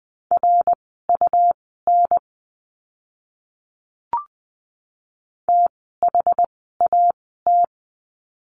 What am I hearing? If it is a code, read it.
Morse code, heard as a keyed tone, about 20 wpm, spaced as for 15 wpm, 700 Hz. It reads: RUN THAT
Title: 115 Lesson 19f - Letter w - Words 20wpm